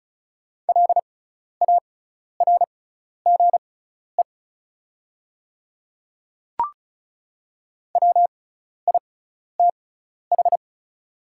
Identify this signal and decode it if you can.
Morse code, heard as a keyed tone, about 36 wpm, spaced as for 12 wpm, 690 Hz. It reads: LARGE WITH
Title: 6x Character Spacing - Top 300 Words - 35wpm